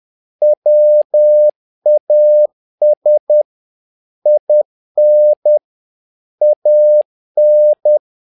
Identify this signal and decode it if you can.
Morse code, heard as a keyed tone, about 10 wpm, 600 Hz. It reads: WAS IN AN
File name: learning center / tower trouble / tower trouble 10wpm